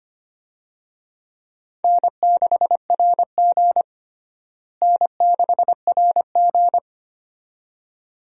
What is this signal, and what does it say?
Morse code, heard as a keyed tone, about 25 wpm, 700 Hz. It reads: N6RG N6RG
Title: USA Call Signs - 4 Characters - Repeated 2x - 25wpm